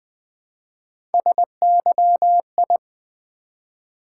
Morse code, heard as a keyed tone, about 20 wpm, 700 Hz. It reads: SYI
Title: Three Letter-Number Combinations - 20wpm